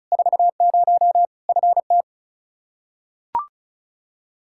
Morse code, heard as a keyed tone, about 34 wpm, 700 Hz. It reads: 40 FT
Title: Practice QSO - Exchange 2 - Info - 35wpm